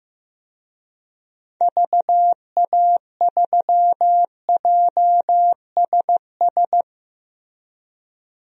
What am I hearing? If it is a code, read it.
Morse code, heard as a keyed tone, about 15 wpm, 700 Hz. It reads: VA3JSS
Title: DXCC-100 Call Signs - 6 Characters - 15wpm